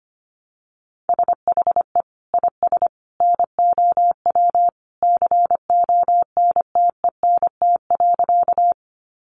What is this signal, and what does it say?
Morse code, heard as a keyed tone, about 25 wpm, 700 Hz. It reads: SHE IS NOW CONTENT.
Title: Sentences from Top 900 Words 25wpm